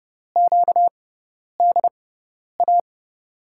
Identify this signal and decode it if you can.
Morse code, heard as a keyed tone, about 32 wpm, spaced as for 10 wpm, 700 Hz. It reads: QDA
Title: USA Call Signs - 6x Character Spacing - 30wpm